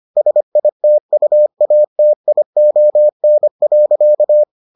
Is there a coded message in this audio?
Yes: Morse code from 0.2 to 4.4 s, heard as a keyed tone, about 25 wpm, 600 Hz. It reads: SITUATION.